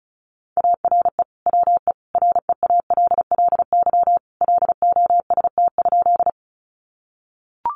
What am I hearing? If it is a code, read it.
Morse code, heard as a keyed tone, about 34 wpm, 700 Hz. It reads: ARE WE REALLY LOST?
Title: Sentences from Top 1400 Words 35wpm